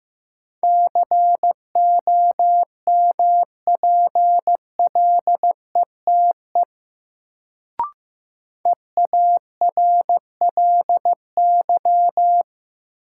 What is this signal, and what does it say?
Morse code, heard as a keyed tone, about 15 wpm, 700 Hz. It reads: COMPLETE EARLY
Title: Sentences from Top 400 Words - New Word Drill - 15wpm